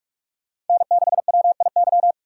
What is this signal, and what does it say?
Morse code, heard as a keyed tone, about 46 wpm, 700 Hz. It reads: N6WIY
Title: USA Call Signs - 5 Characters - 2x Character Spacing - 45wpm